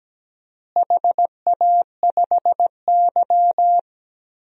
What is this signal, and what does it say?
Morse code, heard as a keyed tone, about 17 wpm, 700 Hz. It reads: HA5Y
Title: DXCC-100 Call Signs - 4 Characters - Repeated 3x - 17wpm